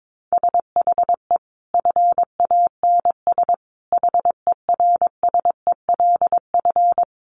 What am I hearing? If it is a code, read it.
Morse code, heard as a keyed tone, about 22 wpm, 700 Hz. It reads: SHE FANS HERSELF
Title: Easy Phrases 22wpm